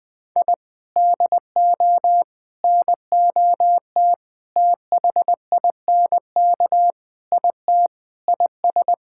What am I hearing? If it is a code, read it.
Morse code, heard as a keyed tone, about 20 wpm, 700 Hz. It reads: I DO NOT THINK IT IS
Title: Sentences with 4-Word Phrases - Spoken Beginning - 20wpm